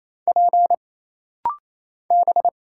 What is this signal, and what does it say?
Morse code, heard as a keyed tone, about 27 wpm, 700 Hz. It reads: P B
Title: Single Letters - Rapid-Fire 28wpm